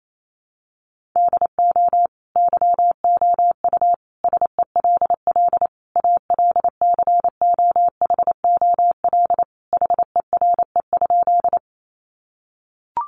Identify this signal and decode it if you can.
Morse code, heard as a keyed tone, about 28 wpm, 700 Hz. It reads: DO YOU SELL ALCOHOL HERE?
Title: Sentences from Top 1400 Words 28wpm